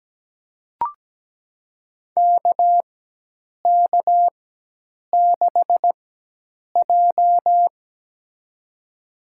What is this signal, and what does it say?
Morse code, heard as a keyed tone, about 17 wpm, spaced as for 8 wpm, 700 Hz. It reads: KK6J
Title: USA Call Signs - 4 Characters - 4x Character Spacing - 17wpm